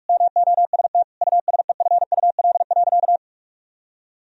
Morse code, heard as a keyed tone, about 41 wpm, 700 Hz. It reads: MOST USEFUL.
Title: Sentences from Top 1000 Words - Review Part 4 - 45wpm